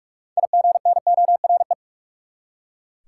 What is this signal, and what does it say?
Morse code, heard as a keyed tone, about 42 wpm, 700 Hz. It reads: IGNORE
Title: Six Letter Words - 45wpm